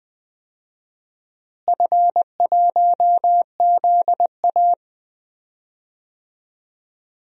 Morse code, heard as a keyed tone, about 20 wpm, 690 Hz. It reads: F1ZA
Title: DXCC-100 Call Signs - 4 Characters - Repeated 3x - 20wpm